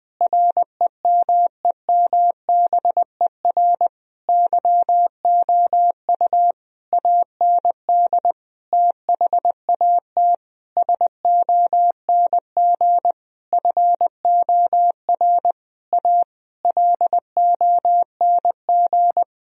Morse code, heard as a keyed tone, about 20 wpm, 700 Hz. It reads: REMEMBER YOU AND THAT SONG FOR A LONG
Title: Sentences from Top 1000 Words - Review Part 2 - 20wpm